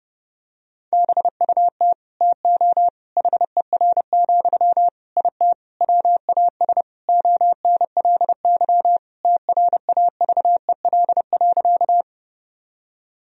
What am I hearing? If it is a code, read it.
Morse code, heard as a keyed tone, about 32 wpm, 700 Hz. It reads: BUT TO HER, IT WAS ONLY TRAVEL.
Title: Sentences from Top 1000 Words - Review Part 2 - 30wpm